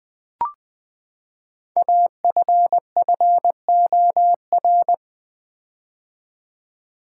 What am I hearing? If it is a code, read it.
Morse code, heard as a keyed tone, about 20 wpm, 700 Hz. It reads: AFFOR
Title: Build A Word - 5 Letters At A Time - 10-Letter Words - 20wpm